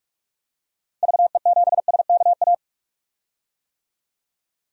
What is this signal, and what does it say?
Morse code, heard as a keyed tone, about 46 wpm, 700 Hz. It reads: VE7SKA
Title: DXCC-100 Call Signs - 6 Characters - 45wpm